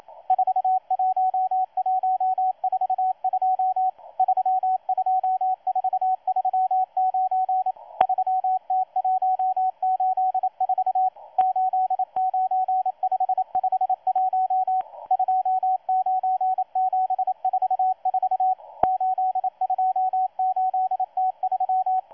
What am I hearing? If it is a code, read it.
Morse code, heard as a keyed tone, about 27 wpm, 750 Hz, with radio noise. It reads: 41142 32439 3T184 89551 29744 828T3
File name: M12-2016-07-07-1310utc-13926khz-error-restart